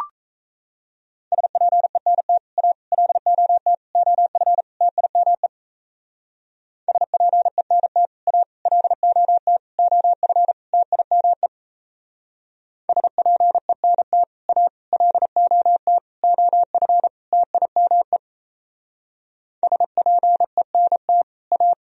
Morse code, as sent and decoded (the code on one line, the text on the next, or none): ... .--. -. - / .- / .-.. --- - / --- ..-. / - .. -- . / ... .--. -. - / .- / .-.. --- - / --- ..-. / - .. -- . / ... .--. -. - / .- / .-.. --- - / --- ..-. / - .. -- . / ... .--. -. - / .-
SPENT A LOT OF TIME SPENT A LOT OF TIME SPENT A LOT OF TIME SPENT A